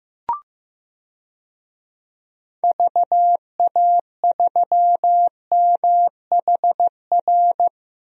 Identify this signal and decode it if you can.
Morse code, heard as a keyed tone, about 15 wpm, 700 Hz. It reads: VA3MHR
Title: DXCC-100 Call Signs - 6 Characters - 15wpm